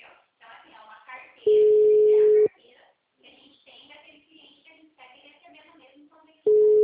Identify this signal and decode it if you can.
Ringback tone (Europe, 425 Hz), with speech.
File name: out- -2004- - - .6